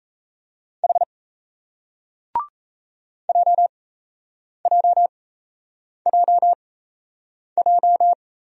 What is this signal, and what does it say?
Morse code, heard as a keyed tone, about 34 wpm, spaced as for 8 wpm, 700 Hz. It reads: H JJJJ
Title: Speed-Racing Single Letters 28wpm